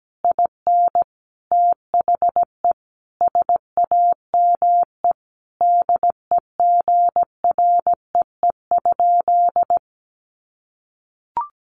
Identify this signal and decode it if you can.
Morse code, heard as a keyed tone, about 17 wpm, 700 Hz. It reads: IN THE SAME DEGREE?
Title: Sentences from Top 1500 Words 17wpm